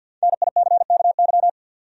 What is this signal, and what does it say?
Morse code, heard as a keyed tone, about 51 wpm, 700 Hz. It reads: NICKY
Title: USA Common Names - 50wpm